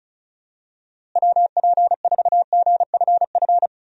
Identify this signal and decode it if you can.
Morse code, heard as a keyed tone, about 34 wpm, 700 Hz. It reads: WP4GFF
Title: USA Call Signs - 6 Characters - Repeated 3x - 35wpm